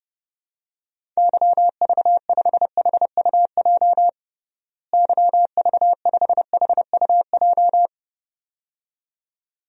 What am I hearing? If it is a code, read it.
Morse code, heard as a keyed tone, about 32 wpm, 700 Hz. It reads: YV5HUJ YV5HUJ
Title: DXCC-100 Call Signs - 6 Characters - Repeated 2x - 30wpm